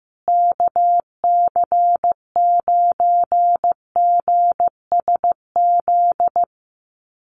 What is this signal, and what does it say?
Morse code, heard as a keyed tone, about 15 wpm, 700 Hz. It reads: KC9GSZ
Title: USA Call Signs Repeated 3x 15wpm